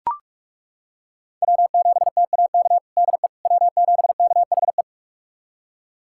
Morse code, heard as a keyed tone, about 41 wpm, 700 Hz. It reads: W7TAK DE W7KHE
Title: USA Callsign DE Callsign - 5 Characters - 45wpm